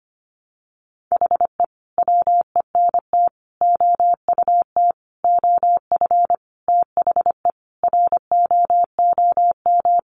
Morse code, heard as a keyed tone, about 25 wpm, 700 Hz. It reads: HE WENT OUT OF THE ROOM
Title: Sentences from Top 1100 Words 25wpm